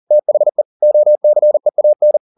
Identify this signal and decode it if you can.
Morse code, heard as a keyed tone, about 40 wpm, 600 Hz. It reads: THE OCEAN